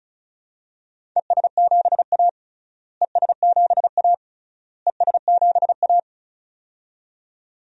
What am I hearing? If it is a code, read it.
Morse code, heard as a keyed tone, about 34 wpm, 700 Hz. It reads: ES7A ES7A ES7A
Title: DXCC-100 Call Signs - 4 Characters - Repeated 3x - 35wpm